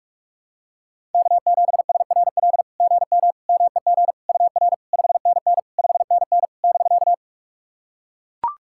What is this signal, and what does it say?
Morse code, heard as a keyed tone, about 44 wpm, 690 Hz. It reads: K7SRL GM MEG UR 5NN 5NN <BK>
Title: SOTA QSO Practice 45wpm